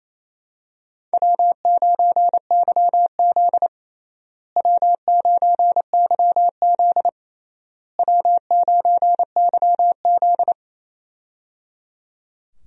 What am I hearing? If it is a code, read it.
Morse code, heard as a keyed tone, about 28 wpm, 700 Hz. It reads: W9YZ W9YZ W9YZ
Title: USA Call Signs - 4 Characters - Repeated 3x - 28wpm